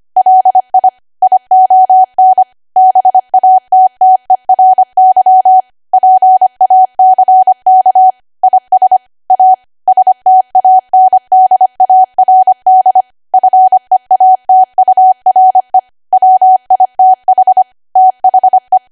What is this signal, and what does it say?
Morse code, heard as a keyed tone, about 25 wpm, 750 Hz. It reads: LI ION BATTERY PACK IS A STANDARD FEATURE WITH THE
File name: Archive / 25 WPM / 25WPM